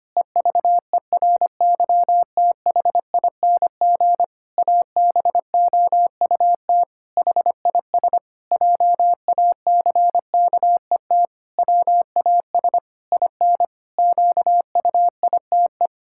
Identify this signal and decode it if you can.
Morse code, heard as a keyed tone, about 25 wpm, 700 Hz. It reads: EVERYTHING ABOUT HIS JACKET WAS IN QUITE
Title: Pangram Challenge v2 - 25wpm